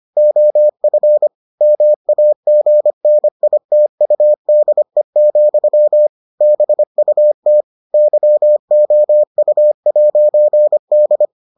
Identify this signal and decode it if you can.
Morse code, heard as a keyed tone, about 25 wpm, 600 Hz. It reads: OF MAGNITUDE, BUT YOU'D